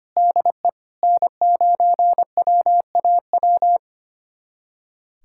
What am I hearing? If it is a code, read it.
Morse code, heard as a keyed tone, about 25 wpm, 700 Hz. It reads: DE N9WAW
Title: USA Callsign DE Callsign 25wpm